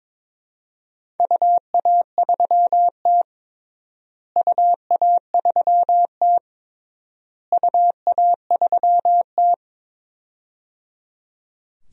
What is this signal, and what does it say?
Morse code, heard as a keyed tone, about 22 wpm, 700 Hz. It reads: UA3T UA3T UA3T
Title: DXCC-100 Call Signs - 4 Characters - Repeated 3x - 22wpm